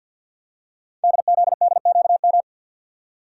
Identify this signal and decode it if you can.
Morse code, heard as a keyed tone, about 51 wpm, 700 Hz. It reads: DZDQM
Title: Code Groups - 5 Letters - 50wpm